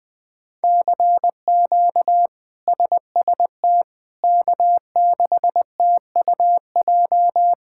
Morse code, heard as a keyed tone, about 20 wpm, 700 Hz. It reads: CQ SST K6TUJ
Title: K1USN SST QSO Practice - 20wpm